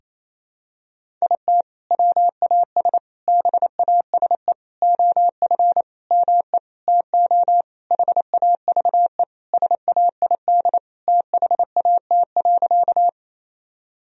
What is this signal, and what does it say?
Morse code, heard as a keyed tone, about 28 wpm, 700 Hz. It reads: IT WAS BASE OF ME TO HAVE SAID THAT.
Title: Sentences from Top 1000 Words - Review Part 1 - 28wpm